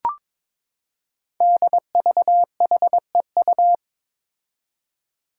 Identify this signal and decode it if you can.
Morse code, heard as a keyed tone, about 22 wpm, 700 Hz. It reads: DVHEU
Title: Five Challenging Letter-Number Combinations 22wpm